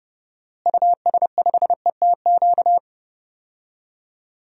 Morse code, heard as a keyed tone, about 30 wpm, 700 Hz. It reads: US5ETQ
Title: DXCC-100 Call Signs - 6 Characters - Repeated 3x - 30wpm